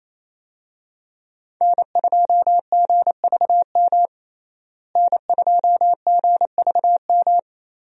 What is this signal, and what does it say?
Morse code, heard as a keyed tone, about 28 wpm, 700 Hz. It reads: N2GVM N2GVM
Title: USA Call Signs - 5 Characters - Repeated 2x - 28wpm